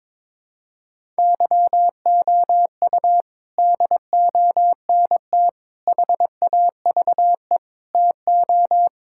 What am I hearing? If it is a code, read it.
Morse code, heard as a keyed tone, about 22 wpm, 700 Hz. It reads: YOU DONT HAVE TO
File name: practice / 22wpm / Sets of 8 Words - 22wpm